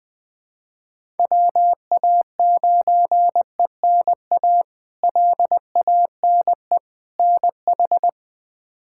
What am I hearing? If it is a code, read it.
Morse code, heard as a keyed tone, about 20 wpm, 700 Hz. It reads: WA9ENA LANE NH
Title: K1USN SST QSO Practice - 20wpm